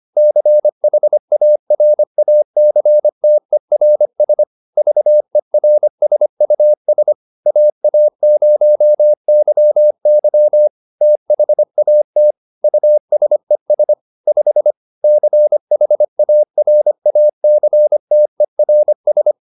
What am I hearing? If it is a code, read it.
Morse code, heard as a keyed tone, about 25 wpm, 600 Hz. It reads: CHARACTERS VERSUS AA0YY THAT USES 5 CHARACTERS
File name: SS220 25